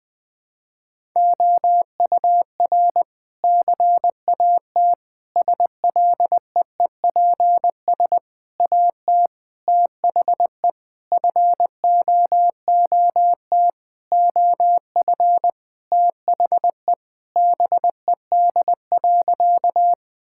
Morse code, heard as a keyed tone, about 20 wpm, 700 Hz. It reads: OUR CAT SLEEPS AT THE FOOT OF THE BED.
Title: Sentences with 5-Word Phrases - Spoken Beginning - 20wpm